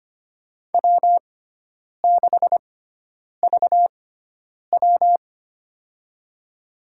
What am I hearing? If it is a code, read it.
Morse code, heard as a keyed tone, about 25 wpm, spaced as for 9 wpm, 700 Hz. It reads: W6VW
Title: USA Call Signs - 4 Characters - 6x Character Spacing - 25wpm